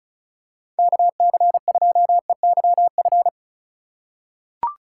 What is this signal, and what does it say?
Morse code, heard as a keyed tone, about 34 wpm, 700 Hz. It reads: KC2EYF
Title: USA Call Signs - 6 Characters - 2x Character Spacing - 35wpm